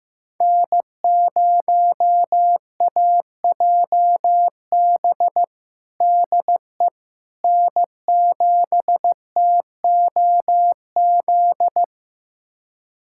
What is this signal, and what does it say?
Morse code, heard as a keyed tone, about 15 wpm, 700 Hz. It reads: N0AJB DE N7TOZ
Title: USA Callsign DE Callsign - 5 Characters - 15wpm